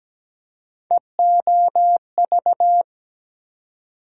Morse code, heard as a keyed tone, about 17 wpm, 700 Hz. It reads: EOV
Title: USA Call Sign Postfixes - 3 Letters 17wpm